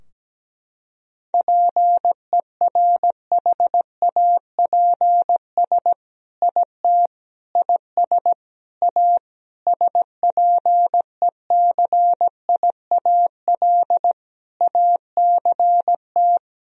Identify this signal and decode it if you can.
Morse code, heard as a keyed tone, about 17 wpm, 700 Hz. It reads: PERHAPS IT IS A SPECIAL ACT
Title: Sentences from Top 1000 Words - Review Part 3 - 17wpm